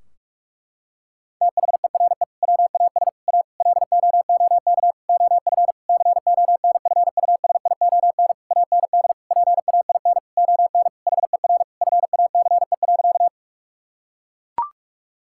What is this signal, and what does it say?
Morse code, heard as a keyed tone, about 41 wpm, 700 Hz. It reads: THERE WAS A LOOK OF CONFUSION AND PAIN ON HER FACE.
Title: Sentences from Top 1600 Words 45wpm